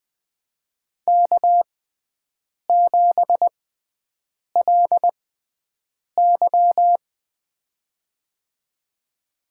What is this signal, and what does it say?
Morse code, heard as a keyed tone, about 20 wpm, spaced as for 7 wpm, 700 Hz. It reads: K7LY
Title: USA Call Signs - 4 Characters - 6x Character Spacing - 20wpm